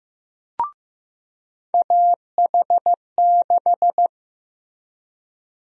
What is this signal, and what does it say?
Morse code, heard as a keyed tone, about 15 wpm, 700 Hz. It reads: AH6
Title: USA Call Sign Prefixes - Repeated 2x - 15wpm